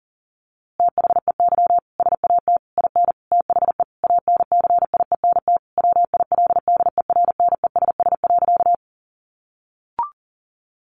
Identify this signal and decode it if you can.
Morse code, heard as a keyed tone, about 40 wpm, 700 Hz. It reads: THEY SAT IN THE ANCIENT WILDERNESS.
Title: A Sound Of Thunder 40wpm